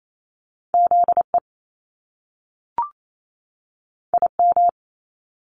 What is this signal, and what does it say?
Morse code, heard as a keyed tone, about 28 wpm, 700 Hz. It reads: ZE IM
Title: Top Two Letter Combinations in Words 28wpm